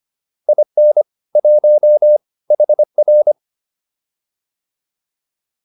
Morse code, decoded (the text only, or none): IN 1 HR